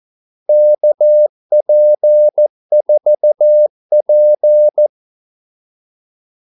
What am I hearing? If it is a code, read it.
Morse code, heard as a keyed tone, about 14 wpm, 600 Hz. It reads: KP4P
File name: Callsigns-14wpm